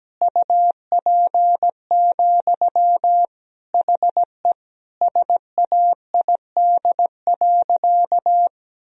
Morse code, heard as a keyed tone, about 17 wpm, 700 Hz. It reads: UP, HE SAID.
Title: Sentences from Top 1000 Words - Review Part 1 - 17wpm